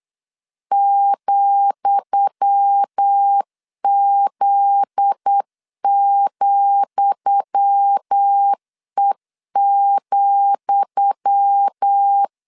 Morse code, heard as a keyed tone, about 8 wpm, 800 Hz. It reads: ,Z,E,
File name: LEZIONI / 10 SIGNS / 10D COMA 8wpm 3749